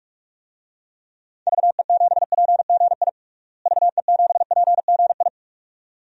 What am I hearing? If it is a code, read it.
Morse code, heard as a keyed tone, about 41 wpm, 690 Hz. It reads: VE7PGI VE7PGI
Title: DXCC-100 Call Signs - 6 Characters - Repeated 2x - 45wpm